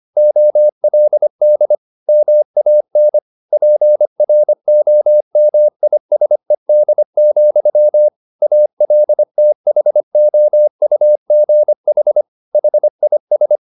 Morse code, heard as a keyed tone, about 25 wpm, 600 Hz. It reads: OLD MAN PROMISED, ALTHOUGH HIS